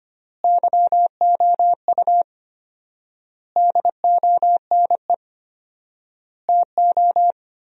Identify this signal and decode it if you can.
Morse code, heard as a keyed tone, about 25 wpm, 700 Hz. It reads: YOU DONE TO
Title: 4x Word-Spacing - Sentences from Top 500 Words - 25wpm